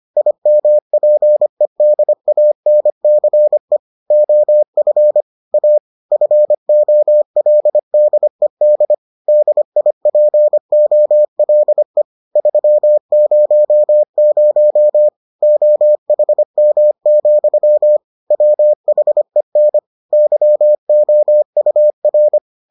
Morse code, heard as a keyed tone, about 25 wpm, 600 Hz. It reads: IMPEDANCE OF A FOLDED DIPOLE 300 OHM, WHEN YOUR